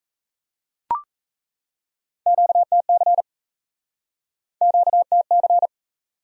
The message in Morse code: QTC QTC

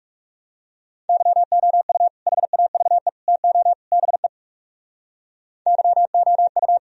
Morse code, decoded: YOU HAVE TO BE YOU

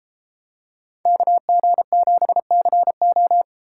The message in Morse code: KG7CO

